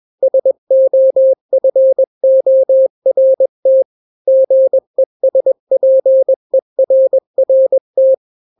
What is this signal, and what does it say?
Morse code, heard as a keyed tone, about 21 wpm, 530 Hz. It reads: SOFORT GESPERRT
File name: daily / 20wpm